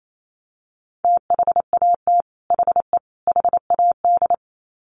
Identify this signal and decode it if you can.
Morse code, heard as a keyed tone, about 28 wpm, 700 Hz. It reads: THAT HE HAD
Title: Sets of 3 Words 28wpm